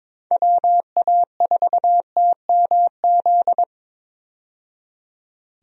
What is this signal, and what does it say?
Morse code, heard as a keyed tone, about 22 wpm, 700 Hz. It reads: WA4TMZ
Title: USA Call Signs - 6 Characters - 22wpm